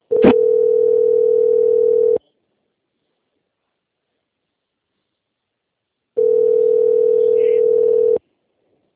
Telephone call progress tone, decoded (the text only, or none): ringback tone